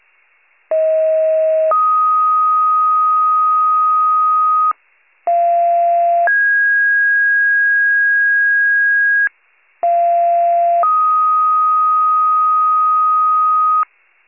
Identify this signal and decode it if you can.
Two-tone page, heard as a one-second tone, then a three-second tone, about 640 then 1250 Hz, about 690 then 1650 Hz, about 690 then 1250 Hz, with noise. It is heard from 0.7 to 13.8 s.
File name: pager / 46 RESCUE DUTY / wcalert